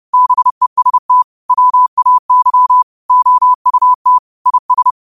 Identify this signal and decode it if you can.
Morse code, heard as a keyed tone, about 30 wpm, 1000 Hz. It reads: BEST WAY OUT IS